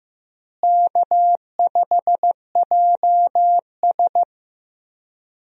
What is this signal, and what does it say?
Morse code, heard as a keyed tone, about 14 wpm, 700 Hz. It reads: K5JS
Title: USA Call Signs - 4 Characters - Repeated 3x - 15wpm